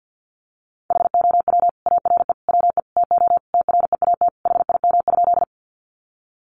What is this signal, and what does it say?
Morse code, heard as a keyed tone, about 50 wpm, 700 Hz. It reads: HOW ARE WE TO TREAT HIM?
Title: Sentences from Top 1300 Words 50wpm